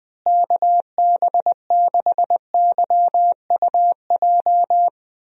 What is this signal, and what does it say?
Morse code, heard as a keyed tone, about 20 wpm, 700 Hz. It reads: KB6YUJ
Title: USA Call Signs - 6 Characters - Repeated 2x - 20wpm